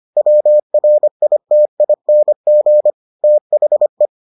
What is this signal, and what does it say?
Morse code, heard as a keyed tone, about 25 wpm, 600 Hz. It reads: WRITING THE